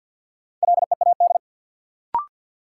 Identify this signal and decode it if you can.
Morse code, heard as a keyed tone, about 52 wpm, 700 Hz. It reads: LEAD
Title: Sentences from Top 500 Words - New Word Drill - 50wpm